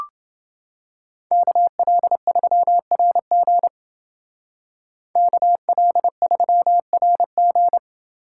Morse code, heard as a keyed tone, about 30 wpm, 700 Hz. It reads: KL3RG KL3RG